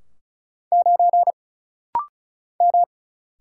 Morse code, heard as a keyed tone, about 34 wpm, 700 Hz. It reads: 9 M
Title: Single Letter-Number Alt - Rapid-Fire 35wpm